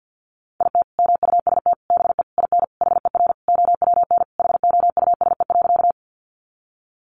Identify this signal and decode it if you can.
Morse code, heard as a keyed tone, about 48 wpm, 700 Hz. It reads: IT MUST BE IN HER OWN HOUSE.
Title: Sentences from Top 300 Words 50wpm